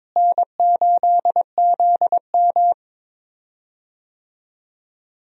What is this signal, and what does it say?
Morse code, heard as a keyed tone, about 22 wpm, 700 Hz. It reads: N8ZM
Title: USA Call Signs - 4 Characters - Repeated 3x - 22wpm